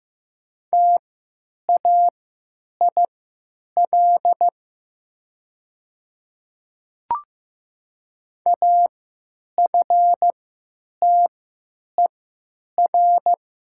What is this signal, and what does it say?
Morse code, heard as a keyed tone, about 15 wpm, spaced as for 9 wpm, 700 Hz. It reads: TAIL AFTER
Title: 3x Character Spacing - Top 500 Words - 15wpm